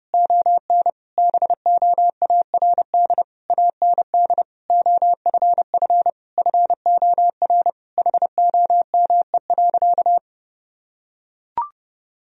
Morse code, heard as a keyed tone, about 30 wpm, 700 Hz. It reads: ON BOARD AND OFF FOR HOME.
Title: Sentences from Top 1000 Words - Review Part 4 - 30wpm